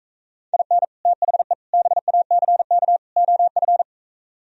Morse code, heard as a keyed tone, about 45 wpm, 700 Hz. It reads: IN THE BACK OF